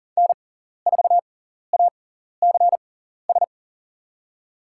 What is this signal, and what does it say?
Morse code, heard as a keyed tone, about 40 wpm, spaced as for 14 wpm, 700 Hz. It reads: N4ACS
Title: USA Call Signs - 6x Character Spacing - 40wpm